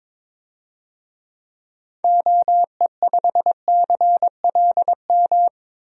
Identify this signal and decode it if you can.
Morse code, heard as a keyed tone, about 21 wpm, 700 Hz. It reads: OE5CLM
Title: DXCC-100 Call Signs - 6 Characters - 22wpm